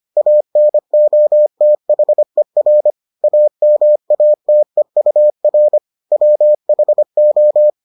Morse code, heard as a keyed tone, about 25 wpm, 600 Hz. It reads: ANOTHER AMATEUR WHO